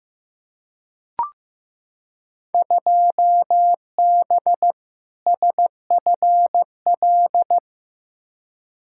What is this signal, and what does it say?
Morse code, heard as a keyed tone, about 15 wpm, 700 Hz. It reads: 2B SFL
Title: Field Day Exchanges Encore - 15wpm